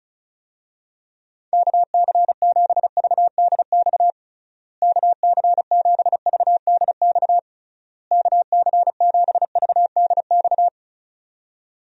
Morse code, heard as a keyed tone, about 34 wpm, 700 Hz. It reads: KC7VDX KC7VDX KC7VDX
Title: USA Call Signs - 6 Characters - Repeated 3x - 35wpm